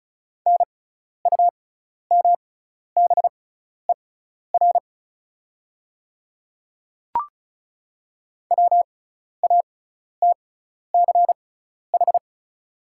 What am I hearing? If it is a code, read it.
Morse code, heard as a keyed tone, about 36 wpm, spaced as for 12 wpm, 700 Hz. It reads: NUMBER WATCH
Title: 6x Character Spacing - Top 300 Words - 35wpm